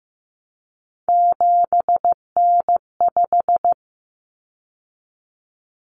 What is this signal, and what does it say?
Morse code, heard as a keyed tone, about 15 wpm, 700 Hz. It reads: 7N5
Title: Common Cut Numbers - 3-Digit 15wpm